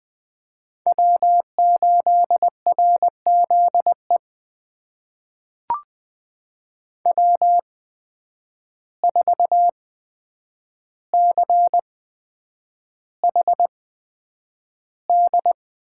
Morse code, heard as a keyed tone, about 20 wpm, spaced as for 9 wpm, 700 Hz. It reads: W8RZE W4CHD
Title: USA Call Signs - 5 Characters - 8x Character Spacing - 20wpm